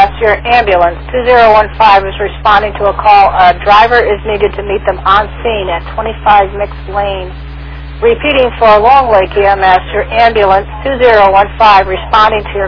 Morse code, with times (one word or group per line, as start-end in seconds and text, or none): none